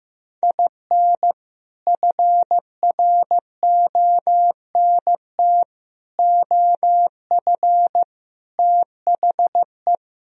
Morse code, heard as a keyed tone, about 15 wpm, 700 Hz. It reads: IN FRONT OF THE
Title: Sentences from Top 1000 Words - Review Part 4 - 15wpm